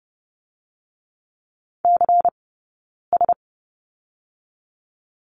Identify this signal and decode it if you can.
Morse code, heard as a keyed tone, about 30 wpm, 700 Hz. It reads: C S
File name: course / 30:10wpm / 099 Lesson 17d - Letter u - Two Characters 30wpm